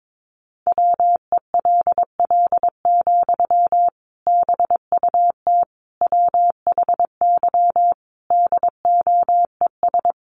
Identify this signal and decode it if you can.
Morse code, heard as a keyed tone, about 22 wpm, 700 Hz. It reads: WELL, BUT WHY DOES
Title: Sentences from Top 700 Words 22wpm